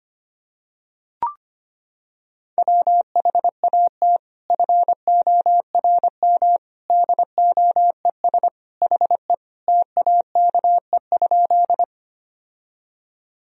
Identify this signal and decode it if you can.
Morse code, heard as a keyed tone, about 25 wpm, 700 Hz. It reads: WHAT FORM DOES HE TAKE?
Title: Sentences from Top 1000 Words - Review Part 1 - 25wpm